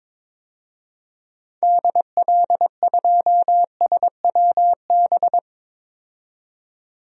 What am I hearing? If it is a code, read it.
Morse code, heard as a keyed tone, about 22 wpm, 700 Hz. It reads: DL2SWB
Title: DXCC-100 Call Signs - 6 Characters - 22wpm